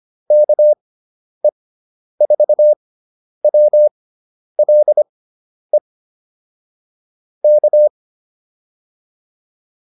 Morse code, heard as a keyed tone, about 25 wpm, spaced as for 10 wpm, 600 Hz. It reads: KE4WLE K